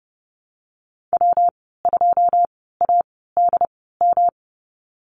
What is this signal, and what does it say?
Morse code, heard as a keyed tone, about 30 wpm, 700 Hz. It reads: W 2 A D M
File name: Lesson 19g - Letter w - Call Signs 30wpm